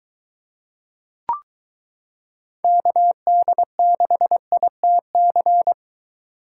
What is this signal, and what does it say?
Morse code, heard as a keyed tone, about 22 wpm, 700 Hz. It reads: KD6ITC